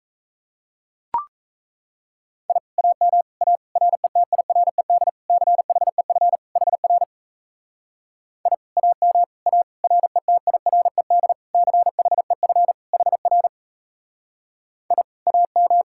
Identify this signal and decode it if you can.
Morse code, heard as a keyed tone, about 37 wpm, 700 Hz. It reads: I AM A RETIRED CHEF HR I AM A RETIRED CHEF HR I AM